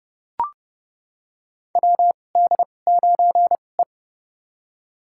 Morse code, heard as a keyed tone, about 30 wpm, spaced as for 22 wpm, 700 Hz. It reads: WD9E
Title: USA Call Signs - 4 Characters - 2x Character Spacing - 30wpm